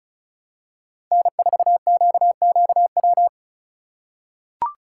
Morse code, heard as a keyed tone, about 34 wpm, 700 Hz. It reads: N4QQW
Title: USA Call Signs - 5 Characters - 3x Character Spacing - 35wpm